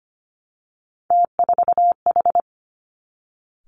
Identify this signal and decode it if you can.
Morse code, heard as a keyed tone, about 25 wpm, 700 Hz. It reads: T4H
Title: Three Challenging Letter-Number Combinations 25wpm